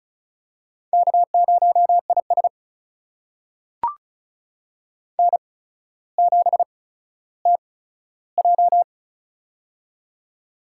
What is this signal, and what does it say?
Morse code, heard as a keyed tone, about 36 wpm, spaced as for 15 wpm, 700 Hz. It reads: K0IS N7TJ
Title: USA Call Signs - 4 Characters - 8x Character Spacing - 35wpm